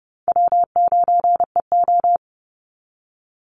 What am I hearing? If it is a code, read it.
Morse code, heard as a keyed tone, about 30 wpm, 700 Hz. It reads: W9EO
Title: Call Signs Repeated 3x - 30wpm